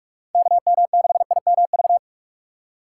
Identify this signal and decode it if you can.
Morse code, heard as a keyed tone, about 46 wpm, 700 Hz. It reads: KM6IMV
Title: USA Call Signs - 6 Characters - 6x Character Spacing - 45wpm